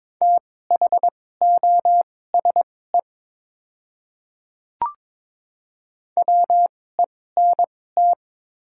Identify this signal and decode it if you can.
Morse code, heard as a keyed tone, about 22 wpm, spaced as for 16 wpm, 700 Hz. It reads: THOSE WENT
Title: 2x Character Spacing - Top 500 Words - 22wpm